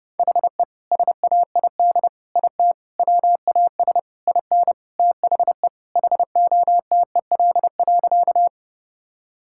Morse code, heard as a keyed tone, about 30 wpm, 700 Hz. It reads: HE SAID IT WAS IN THE HOTEL.
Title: Sentences from Top 1600 Words 30wpm